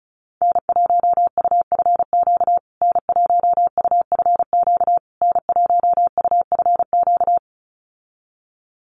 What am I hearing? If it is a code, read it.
Morse code, heard as a keyed tone, about 34 wpm, 700 Hz. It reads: N1UFQ N1UFQ N1UFQ
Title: USA Call Signs Repeated 3x 35wpm